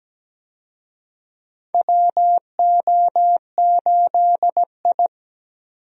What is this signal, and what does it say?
Morse code, heard as a keyed tone, about 17 wpm, 700 Hz. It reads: WO8I
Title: USA Call Signs - 4 Characters - Repeated 3x - 17wpm